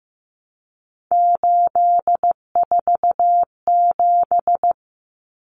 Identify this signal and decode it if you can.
Morse code, heard as a keyed tone, about 15 wpm, 700 Hz. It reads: 847
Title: Numbers - 3-Digit 15wpm